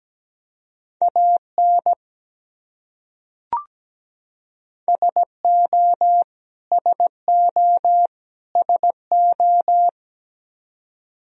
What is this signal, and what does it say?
Morse code, heard as a keyed tone, about 17 wpm, 700 Hz. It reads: AN SO SO SO
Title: Top 100 Words Repeated 17wpm